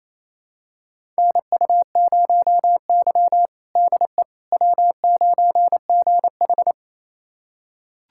Morse code, heard as a keyed tone, about 28 wpm, 700 Hz. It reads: NU0Y DE W9GH
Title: USA Callsign DE Callsign - 4 Characters - 28wpm